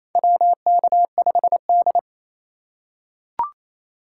Morse code, heard as a keyed tone, about 27 wpm, 700 Hz. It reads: WK5D
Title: USA Call Signs - 4x Character Spacing - 28wpm